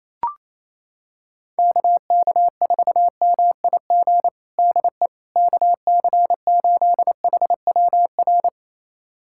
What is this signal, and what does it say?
Morse code, heard as a keyed tone, about 28 wpm, 700 Hz. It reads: KK4MIG DE KC8HWR
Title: USA Callsign DE Callsign 28wpm